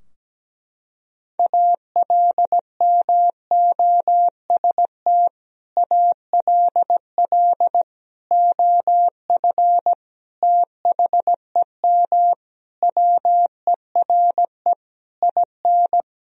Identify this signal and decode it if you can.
Morse code, heard as a keyed tone, about 17 wpm, 700 Hz. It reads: ALMOST ALL OF THEM WERE IN
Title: Sentences from Top 1000 Words - Review Part 4 - 17wpm